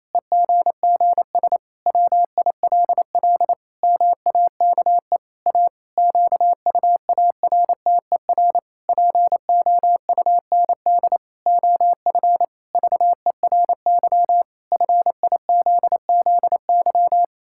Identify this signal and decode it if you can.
Morse code, heard as a keyed tone, about 28 wpm, 700 Hz. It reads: EGGS WILL MAKE A QUARTER POUND OF VERY FIZZY
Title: Pangram Challenge v2 - 28wpm